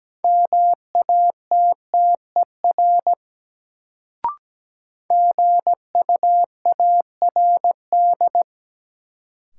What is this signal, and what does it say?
Morse code, heard as a keyed tone, about 17 wpm, 700 Hz. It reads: MATTER GUARD
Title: Sentences from Top 700 Words - New Word Drill - 17wpm